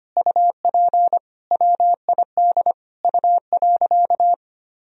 Morse code, heard as a keyed tone, about 25 wpm, 700 Hz. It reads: UP WID U.
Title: Sentences from Top 300 Words and Abbreviations 25wpm